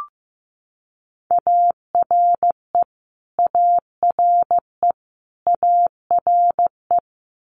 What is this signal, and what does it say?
Morse code, heard as a keyed tone, about 15 wpm, 700 Hz. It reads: ARE ARE ARE
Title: Multi-letter Phonemes Repeated 3x 15wpm